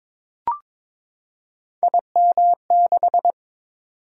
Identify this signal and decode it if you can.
Morse code, heard as a keyed tone, about 23 wpm, 700 Hz. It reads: IM6
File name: practice / 22wpm / Three Letter-Number Combinations - 22wpm